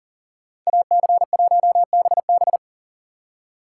Morse code, heard as a keyed tone, about 49 wpm, 700 Hz. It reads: AC1BB
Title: Speed Racing - USA 2x2 Call Signs 40wpm